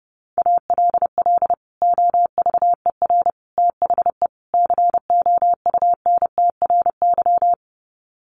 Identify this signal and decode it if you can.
Morse code, heard as a keyed tone, about 28 wpm, 700 Hz. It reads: ALL OVER THE COUNTRY
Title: Sets of 4 Words 30wpm